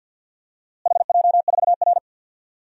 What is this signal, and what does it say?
Morse code, heard as a keyed tone, about 52 wpm, 700 Hz. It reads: HJ4R
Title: DXCC-100 Call Signs - 4 Characters - Repeated 2x - 50wpm